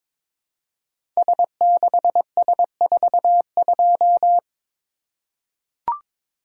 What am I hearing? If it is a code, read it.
Morse code, heard as a keyed tone, about 22 wpm, 700 Hz. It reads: S6S42
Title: Code Groups - 5 Letter:Numbers - 22wpm